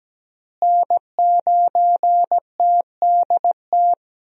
Morse code, heard as a keyed tone, about 17 wpm, 700 Hz. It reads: N9TDT
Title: USA Call Signs - 5 Characters - Repeated 3x - 17wpm